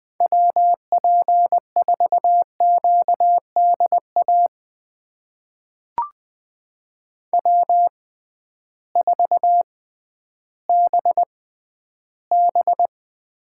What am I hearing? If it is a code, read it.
Morse code, heard as a keyed tone, about 20 wpm, spaced as for 12 wpm, 700 Hz. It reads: WP4QDA W4BB
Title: USA Call Signs - 6x Character Spacing - 20wpm